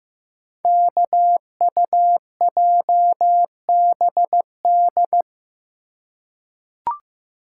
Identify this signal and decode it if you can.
Morse code, heard as a keyed tone, about 15 wpm, 700 Hz. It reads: KUJBD
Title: Code Groups - 5 Letter:Numbers - 15wpm